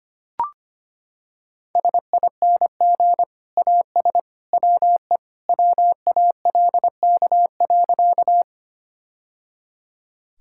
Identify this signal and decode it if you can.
Morse code, heard as a keyed tone, about 25 wpm, 690 Hz. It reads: SING AS WE WALK.
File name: practice / 25wpm / Sentences with 2-Word Phrases - 25wpm